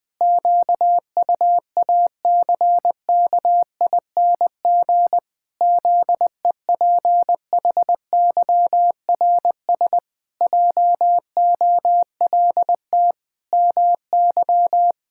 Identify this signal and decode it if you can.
Morse code, heard as a keyed tone, about 20 wpm, 700 Hz. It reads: QUACKING ZEPHYRS JOLT MY
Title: Pangram Challenge v2 - 20wpm